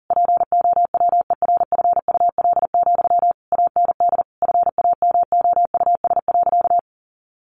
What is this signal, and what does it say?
Morse code, heard as a keyed tone, about 37 wpm, 700 Hz. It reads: POWERFUL, AND FAMOUS.
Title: Sentences from Top 1500 Words 40wpm